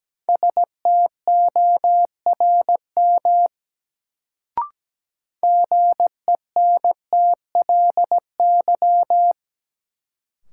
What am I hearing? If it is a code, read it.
Morse code, heard as a keyed tone, about 17 wpm, 700 Hz. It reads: STORM GENTLY
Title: Sentences from Top 1500 Words - New Word Drill Alt - 17wpm